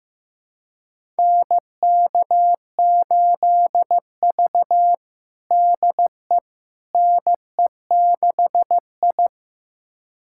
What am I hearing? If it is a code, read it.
Morse code, heard as a keyed tone, about 15 wpm, 700 Hz. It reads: NK8V DE NE6I
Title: USA Callsign DE Callsign - 4 Characters - 15wpm